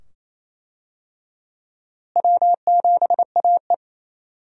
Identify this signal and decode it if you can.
Morse code, heard as a keyed tone, about 28 wpm, 700 Hz. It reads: W7AE
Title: USA Call Signs - 4 Characters - Repeated 2x - 28wpm